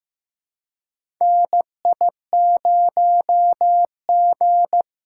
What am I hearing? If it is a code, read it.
Morse code, heard as a keyed tone, about 15 wpm, 700 Hz. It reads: NI0G
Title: USA Call Signs - 4 Characters - 6x Character Spacing - 15wpm